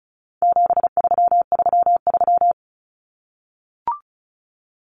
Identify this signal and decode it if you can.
Morse code, heard as a keyed tone, about 34 wpm, 700 Hz. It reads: 7333
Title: Numbers - 4-Digit 35wpm